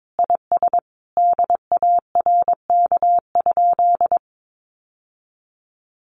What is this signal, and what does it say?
Morse code, heard as a keyed tone, about 22 wpm, 700 Hz. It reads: IS DARK?
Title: Sentences from Top 900 Words 22wpm